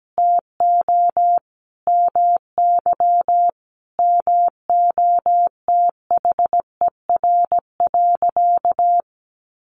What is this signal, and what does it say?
Morse code, heard as a keyed tone, about 17 wpm, 700 Hz. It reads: TO MY MOTHER.
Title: Sentences from Top 1400 Words 17wpm